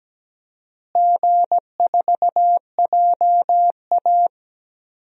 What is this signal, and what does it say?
Morse code, heard as a keyed tone, about 17 wpm, 700 Hz. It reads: G4JA
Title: DXCC-100 Call Signs - 4 Characters - Repeated 2x - 17wpm